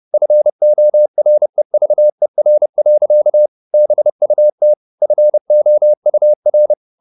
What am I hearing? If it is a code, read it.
Morse code, heard as a keyed tone, about 30 wpm, 600 Hz. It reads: FOREVER. BUT FOUR
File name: the old man and the sea / the old man and the sea 1 30wpm